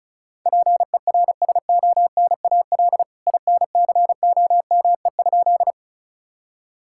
Morse code, heard as a keyed tone, about 36 wpm, 700 Hz. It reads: PERSONAL INCOME?
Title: Sentences from Top 1000 Words - Review Part 5 - 35wpm